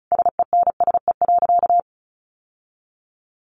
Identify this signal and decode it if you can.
Morse code, heard as a keyed tone, about 34 wpm, 700 Hz. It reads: SENSE.
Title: Sentences from Top 1100 Words 35wpm